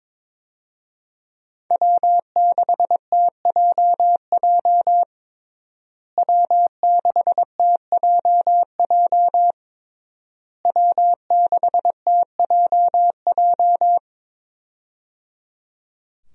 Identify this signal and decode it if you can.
Morse code, heard as a keyed tone, about 22 wpm, 690 Hz. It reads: W6TJJ W6TJJ W6TJJ
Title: USA Call Signs - 5 Characters - Repeated 3x - 22wpm